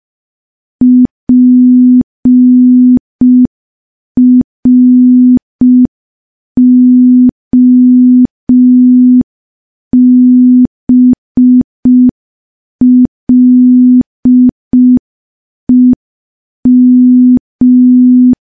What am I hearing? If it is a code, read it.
Morse code, heard as a keyed tone, about 5 wpm, 250 Hz. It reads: PROBLEM
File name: morse / morse-training-05WPM